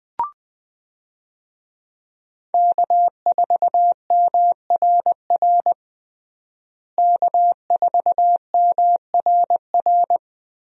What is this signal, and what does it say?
Morse code, heard as a keyed tone, about 20 wpm, 700 Hz. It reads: K4MRR K4MRR
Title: USA Call Signs - 5 Characters - Repeated 3x - 20wpm